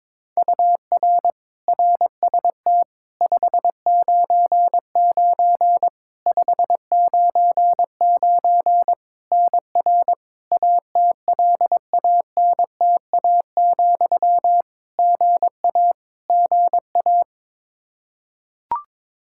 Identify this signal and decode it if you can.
Morse code, heard as a keyed tone, about 22 wpm, 700 Hz. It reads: UR RST 599 599 NR ATLANTA, GA GA
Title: Practice QSO - Exchange 1 - Beginning - 22wpm